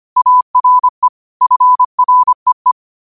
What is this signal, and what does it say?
Morse code, heard as a keyed tone, about 25 wpm, 1000 Hz. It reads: ARE FREE